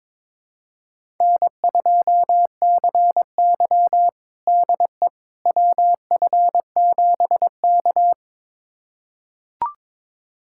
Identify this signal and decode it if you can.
Morse code, heard as a keyed tone, about 22 wpm, 700 Hz. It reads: N2CY DE WF7K
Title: USA Callsign DE Callsign - 4 Characters - 22wpm